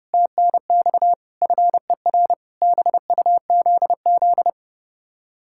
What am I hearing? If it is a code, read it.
Morse code, heard as a keyed tone, about 40 wpm, 700 Hz. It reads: TNX FER BUZZ